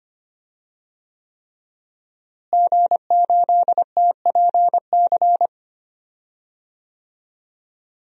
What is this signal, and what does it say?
Morse code, heard as a keyed tone, about 25 wpm, 700 Hz. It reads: G8TPC
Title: DXCC-100 Call Signs - 5 Characters - Repeated 3x - 25wpm